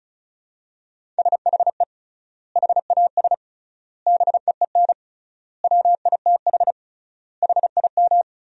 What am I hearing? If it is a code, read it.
Morse code, heard as a keyed tone, about 35 wpm, 700 Hz. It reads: SHE HAS BEEN WITH HIM
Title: 3x Word-Spacing - Sentences from Top 200 Words - 35wpm